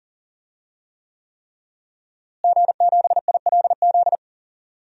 Morse code, heard as a keyed tone, about 40 wpm, 700 Hz. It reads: G7ILZ
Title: DXCC-100 Call Signs - 5 Characters - Repeated 3x - 40wpm